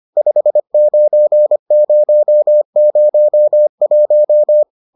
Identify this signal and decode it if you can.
Morse code, heard as a keyed tone, about 25 wpm, 600 Hz. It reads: 59001